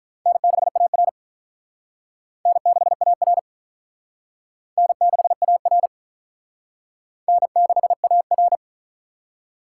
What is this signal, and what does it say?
Morse code, heard as a keyed tone, about 46 wpm, 700 Hz. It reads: N6AR N6AR N6AR N6AR